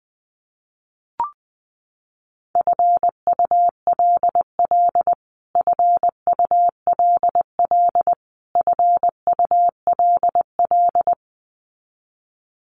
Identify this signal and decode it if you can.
Morse code, heard as a keyed tone, about 20 wpm, 700 Hz. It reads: FULL FULL FULL
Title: Top 500 Words Repeated 20wpm